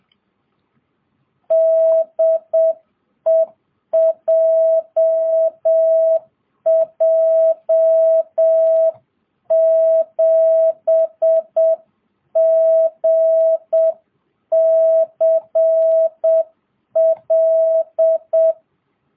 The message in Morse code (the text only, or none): DE JJ7GCL